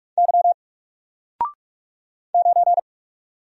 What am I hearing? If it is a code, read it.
Morse code, heard as a keyed tone, about 48 wpm, 700 Hz. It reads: Y 9
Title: Single Character (Letter:Number:Punctuation:Prosign) Alt 45wpm